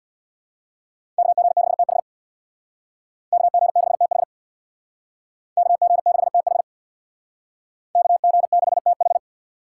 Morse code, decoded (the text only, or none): KG6TH KG6TH KG6TH KG6TH